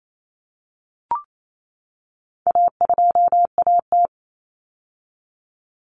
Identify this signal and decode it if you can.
Morse code, heard as a keyed tone, about 28 wpm, 700 Hz. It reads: A2AT
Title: Common Cut Numbers - 4-Digit 28wpm